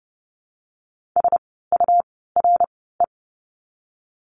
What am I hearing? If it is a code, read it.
Morse code, heard as a keyed tone, about 30 wpm, spaced as for 17 wpm, 700 Hz. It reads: SURE
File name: Lesson 17f - Letter u - Words 30wpm